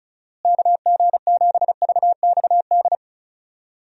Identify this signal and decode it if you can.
Morse code, heard as a keyed tone, about 34 wpm, 700 Hz. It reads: KG7VXD
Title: USA Call Signs - 6 Characters - Repeated 3x - 35wpm